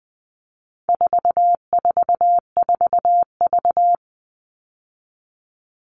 Morse code, heard as a keyed tone, about 20 wpm, 700 Hz. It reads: 444V
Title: All Cut Numbers - 4-Digit 20wpm